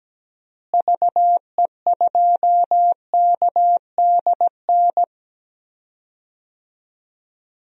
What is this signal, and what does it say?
Morse code, heard as a keyed tone, about 17 wpm, 700 Hz. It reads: VE2KDN
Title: DXCC-100 Call Signs - 6 Characters - Repeated 3x - 17wpm